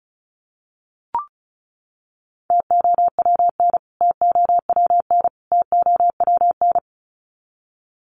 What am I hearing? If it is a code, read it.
Morse code, heard as a keyed tone, about 34 wpm, 700 Hz. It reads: TOWN TOWN TOWN
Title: Top 500 Words Repeated 35wpm